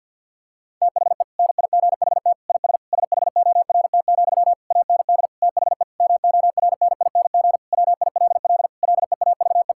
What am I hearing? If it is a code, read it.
Morse code, heard as a keyed tone, about 50 wpm, 700 Hz. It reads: THE NIGHT IS SHORT, AND THE MORNING WILL LEAVE
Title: Sentences from Top 1600 Words 50wpm